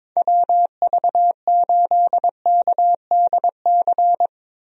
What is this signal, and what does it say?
Morse code, heard as a keyed tone, about 22 wpm, 700 Hz. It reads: WV8KDC
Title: USA Call Signs - 6 Characters - Repeated 2x - 22wpm